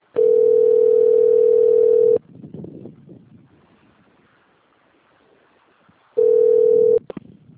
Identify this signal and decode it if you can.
Ringback tone (North America), 440 + 480 Hz.
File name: rg-1600-unknown- - - .96152